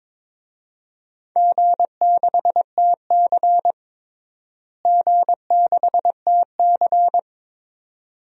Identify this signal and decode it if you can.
Morse code, heard as a keyed tone, about 22 wpm, 700 Hz. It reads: G6TC G6TC
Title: DXCC-100 Call Signs - 4 Characters - Repeated 2x - 22wpm